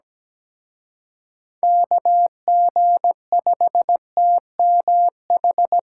Morse code, heard as a keyed tone, about 20 wpm, 700 Hz. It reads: KG5TMH
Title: Speed Racing - USA 2x3 Call Signs 17wpm